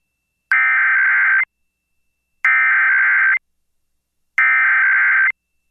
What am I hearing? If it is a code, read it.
Emergency Alert System (SAME) tones (header bursts), heard as harsh buzzing bursts. It is heard from 0.5 to 5.3 s.